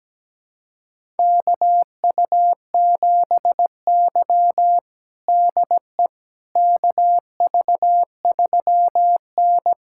Morse code, heard as a keyed tone, about 17 wpm, 700 Hz. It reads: KU7Y DE KV3N
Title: USA Callsign DE Callsign - 4 Characters - 17wpm